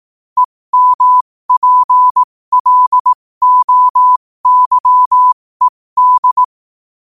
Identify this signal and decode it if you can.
Morse code, heard as a keyed tone, about 18 wpm, spaced as for 15 wpm, 1000 Hz. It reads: EMPLOYED
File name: qotdfiles / 15wpm